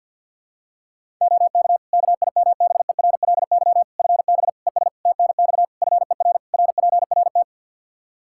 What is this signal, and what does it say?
Morse code, heard as a keyed tone, about 52 wpm, 700 Hz. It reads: OK KIMBERLY FB ES TNX FER RPRT
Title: Practice QSO - Exchange 2 - Info - 50wpm